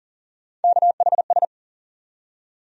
Morse code, heard as a keyed tone, about 40 wpm, 700 Hz. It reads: KHS
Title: USA Call Sign Postfixes - 3 Letters 40wpm